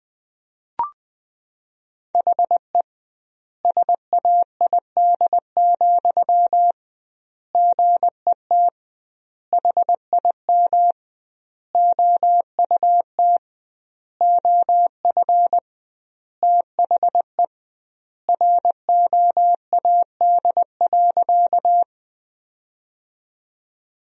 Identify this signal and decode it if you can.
Morse code, heard as a keyed tone, about 20 wpm, 700 Hz. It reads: HE SAID, GET HIM OUT OF THE ROAD.
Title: 2x Word-Spacing - Sentences from Top 400 Words - 20wpm